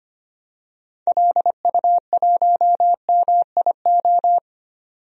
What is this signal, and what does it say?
Morse code, heard as a keyed tone, about 25 wpm, 700 Hz. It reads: LU1MIO
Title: DXCC-100 Call Signs - 6 Characters - 25wpm